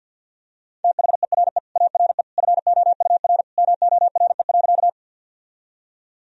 Morse code, heard as a keyed tone, about 46 wpm, 700 Hz. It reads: THERE ARE FOUR MORE.
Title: Sentences from Top 1000 Words - Review Part 1 - 50wpm